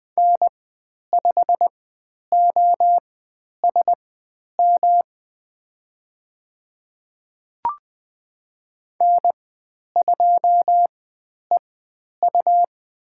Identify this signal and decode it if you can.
Morse code, heard as a keyed tone, about 20 wpm, spaced as for 10 wpm, 700 Hz. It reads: N5OSM N2EU
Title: 109 Lesson 18g - Letter m - Call Signs 20wpm